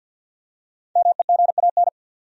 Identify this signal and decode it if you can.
Morse code, heard as a keyed tone, about 47 wpm, 700 Hz. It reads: MEGAN
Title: USA Common Names - Max 5 Letters - 50wpm